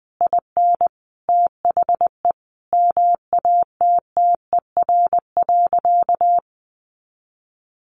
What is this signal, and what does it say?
Morse code, heard as a keyed tone, about 20 wpm, 700 Hz. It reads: IN THE MATTER.
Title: Sentences from Top 1200 Words 20wpm